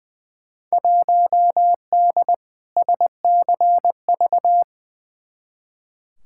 Morse code, heard as a keyed tone, about 21 wpm, 700 Hz. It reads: 1D SCV
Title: Field Day Exchanges Encore - 20wpm